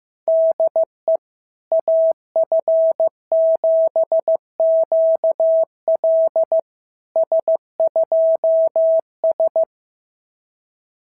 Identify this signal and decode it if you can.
Morse code, heard as a keyed tone, about 16 wpm, 650 Hz. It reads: DE AF7QL S2S
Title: SOTA QSO Practice 15wpm